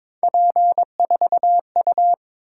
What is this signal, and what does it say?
Morse code, heard as a keyed tone, about 22 wpm, 700 Hz. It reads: P4U
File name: practice / 22wpm / Three Letter-Number Combinations - 22wpm